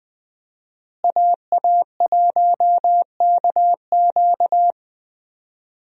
Morse code, heard as a keyed tone, about 20 wpm, 700 Hz. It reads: AA1KQ
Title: USA Call Signs - 5 Characters - Repeated 3x - 20wpm